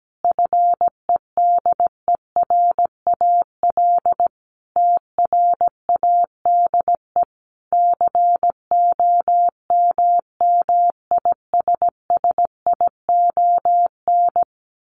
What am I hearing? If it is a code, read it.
Morse code, heard as a keyed tone, about 17 wpm, 700 Hz. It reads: FEDERAL TRADE COMMISSION
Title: Sentences from Top 1400 Words 17wpm